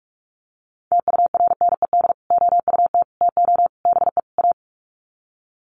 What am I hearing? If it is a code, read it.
Morse code, heard as a keyed tone, about 44 wpm, 700 Hz. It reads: TURNED OUT TO BE A
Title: Sets of 5 Words 45wpm